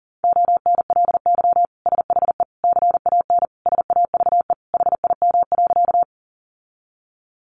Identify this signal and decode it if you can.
Morse code, heard as a keyed tone, about 37 wpm, 700 Hz. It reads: ONLY SHE CAN SAVE HIM.
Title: Sentences from Top 1100 Words 40wpm